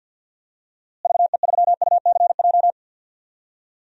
Morse code, heard as a keyed tone, about 51 wpm, 700 Hz. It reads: VE3UCJ
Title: DXCC-100 Call Signs - 6 Characters - Repeated 3x - 50wpm